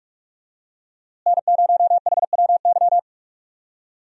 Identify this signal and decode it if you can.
Morse code, heard as a keyed tone, about 42 wpm, 700 Hz. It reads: N0HWY
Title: USA Call Signs - 5 Characters - Repeated 3x - 45wpm